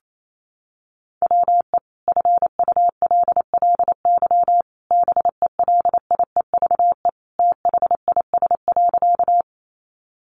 Morse code, heard as a keyed tone, about 28 wpm, 700 Hz. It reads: WE FULLY BELIEVE THIS.
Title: Sentences from Top 1300 Words 28wpm